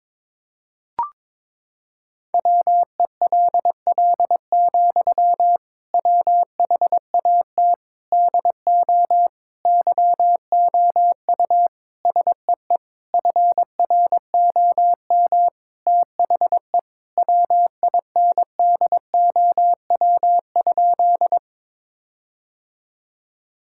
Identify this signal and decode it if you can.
Morse code, heard as a keyed tone, about 22 wpm, 700 Hz. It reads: WELL, WHAT DO YOU SEE FROM THE WINDOW?
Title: Sentences from Top 1600 Words 22wpm